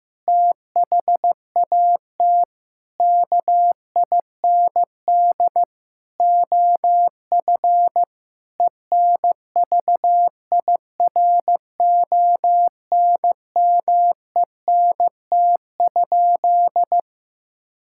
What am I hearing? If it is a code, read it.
Morse code, heard as a keyed tone, about 15 wpm, 700 Hz. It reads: THAT KIND OF ENVIRONMENT?
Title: Sentences from Top 1000 Words - Review Part 5 - 15wpm